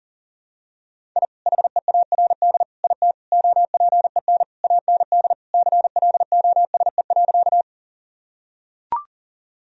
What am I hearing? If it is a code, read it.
Morse code, heard as a keyed tone, about 40 wpm, 700 Hz. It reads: I HEARD IT OPEN AND CLOSE.
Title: Sentences from Top 1000 Words - Review Part 1 - 40wpm